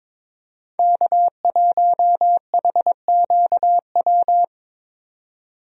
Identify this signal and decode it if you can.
Morse code, heard as a keyed tone, about 22 wpm, 700 Hz. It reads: K1HQW
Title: USA Call Signs - 5 Characters - 22wpm